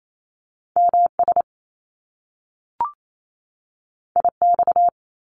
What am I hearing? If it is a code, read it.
Morse code, heard as a keyed tone, about 28 wpm, 700 Hz. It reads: MS IX
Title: Top Two Letter Combinations in Words 28wpm